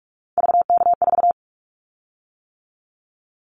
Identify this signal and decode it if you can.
Morse code, heard as a keyed tone, about 44 wpm, 700 Hz. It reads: VK4
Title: Callsign Prefixes for Top 100 DXCC - 45wpm